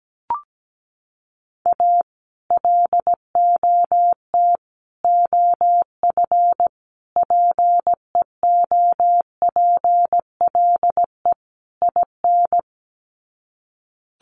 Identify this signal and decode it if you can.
Morse code, heard as a keyed tone, about 17 wpm, 700 Hz. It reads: A LOT OF PEOPLE IN
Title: Sets of 5 Words 17wpm